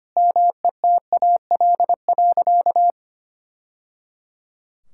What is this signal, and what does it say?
Morse code, heard as a keyed tone, about 25 wpm, 700 Hz. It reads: METAL.
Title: Sentences from Top 1000 Words - Review Part 6 - 25wpm